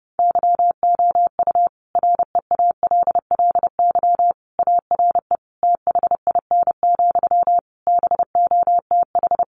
Morse code, heard as a keyed tone, about 30 wpm, 700 Hz. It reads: YOU REALLY ARE THIN, BOTH
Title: Sentences from Top 1200 Words 30wpm